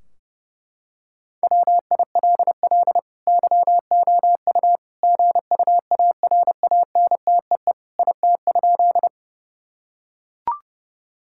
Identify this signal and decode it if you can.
Morse code, heard as a keyed tone, about 32 wpm, 700 Hz. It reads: WILL YOU GUARANTEE IT?
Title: Sentences from Top 1600 Words 30wpm